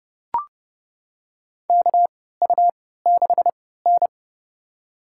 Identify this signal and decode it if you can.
Morse code, heard as a keyed tone, about 30 wpm, spaced as for 17 wpm, 700 Hz. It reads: KU6N
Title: USA Call Signs - 4 Characters - 3x Character Spacing - 30wpm